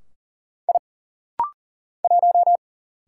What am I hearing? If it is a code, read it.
Morse code, heard as a keyed tone, about 40 wpm, 700 Hz. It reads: I 1
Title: Single Letter-Number - Rapid-Fire 40wpm